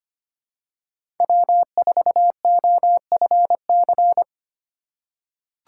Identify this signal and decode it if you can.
Morse code, heard as a keyed tone, about 25 wpm, 700 Hz. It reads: W4OFC
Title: USA Call Signs - 5 Characters - 8x Character Spacing - 25wpm